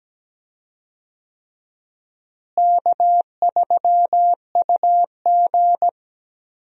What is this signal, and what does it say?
Morse code, heard as a keyed tone, about 17 wpm, 700 Hz. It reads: K3UG
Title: USA Call Signs - 4 Characters - Repeated 3x - 17wpm